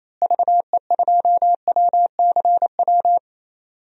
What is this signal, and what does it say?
Morse code, heard as a keyed tone, about 28 wpm, 700 Hz. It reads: VE2WCW
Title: DXCC-100 Call Signs - 6 Characters - Repeated 3x - 28wpm